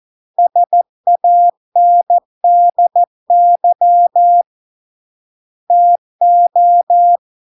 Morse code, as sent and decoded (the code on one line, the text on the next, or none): ... .- -. -.. -.-- / - ---
SANDY TO